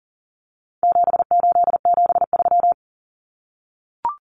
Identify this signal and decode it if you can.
Morse code, heard as a keyed tone, about 40 wpm, 700 Hz. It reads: 7873
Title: Common Cut Numbers - 4-Digit 40wpm